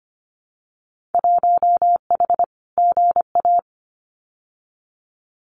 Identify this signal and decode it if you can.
Morse code, heard as a keyed tone, about 25 wpm, 690 Hz. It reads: 1H GA
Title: Winter Field Day Exchanges 25wpm